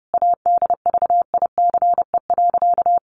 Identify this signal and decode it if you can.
Morse code, heard as a keyed tone, about 28 wpm, 700 Hz. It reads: ADVICE.
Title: Sentences from Top 1100 Words 30wpm